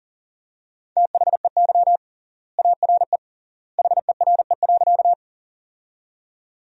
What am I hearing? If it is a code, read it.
Morse code, heard as a keyed tone, about 40 wpm, 700 Hz. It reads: THEY ARE HERE.
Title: 3x Word-Spacing - Sentences from Top 200 Words - 40wpm